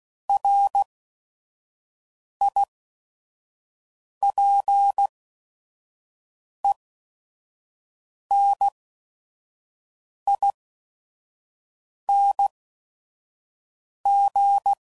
Morse code, heard as a keyed tone, about 16 wpm, spaced as for 5 wpm, 800 Hz. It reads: RIPENING
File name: Prefixes / By First Letter / Rs